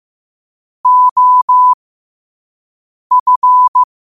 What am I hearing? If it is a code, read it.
Morse code, heard as a keyed tone, about 15 wpm, spaced as for 5 wpm, 1000 Hz. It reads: OF
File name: qotdfiles / 5wpm